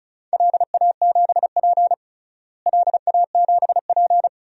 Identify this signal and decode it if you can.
Morse code, heard as a keyed tone, about 34 wpm, 700 Hz. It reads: LA7P LA7P
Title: DXCC-100 Call Signs - 4 Characters - Repeated 2x - 35wpm